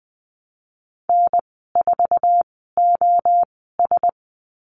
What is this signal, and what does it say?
Morse code, heard as a keyed tone, about 20 wpm, spaced as for 15 wpm, 700 Hz. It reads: N4OS
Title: 048 Lesson 9g - Number 4 - Call Signs 20wpm